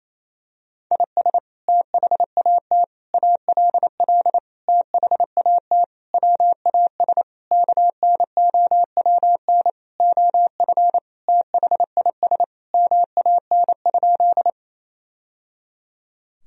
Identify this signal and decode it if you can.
Morse code, heard as a keyed tone, about 27 wpm, 700 Hz. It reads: IS THAT ALL THAT WAS KNOWN OF THIS MAN?
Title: Sentences from Top 1000 Words - Review Part 3 - 28wpm